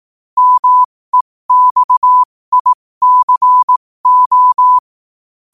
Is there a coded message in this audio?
Yes: Morse code from 0.4 to 4.8 s, heard as a keyed tone, about 18 wpm, spaced as for 15 wpm, 1000 Hz. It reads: MEXICO